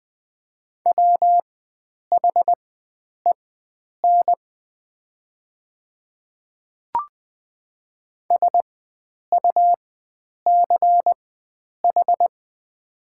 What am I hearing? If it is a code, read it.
Morse code, heard as a keyed tone, about 20 wpm, spaced as for 9 wpm, 700 Hz. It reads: WHEN SUCH
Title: 4x Character Spacing - Top 300 Words - 20wpm